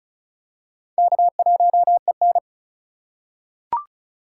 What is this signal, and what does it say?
Morse code, heard as a keyed tone, about 34 wpm, 700 Hz. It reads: K1EN
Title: USA Call Signs - 6x Character Spacing - 35wpm